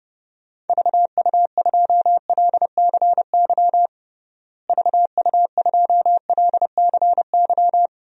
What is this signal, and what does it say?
Morse code, heard as a keyed tone, about 32 wpm, 700 Hz. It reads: VU2LCY VU2LCY
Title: DXCC-100 Call Signs - 6 Characters - Repeated 2x - 30wpm